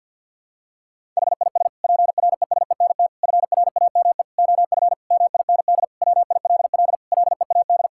Morse code, heard as a keyed tone, about 52 wpm, 700 Hz. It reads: HIS PRESENT FRAME OF MIND WILL LEAD
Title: Sentences from Top 1000 Words - Review Part 6 - 50wpm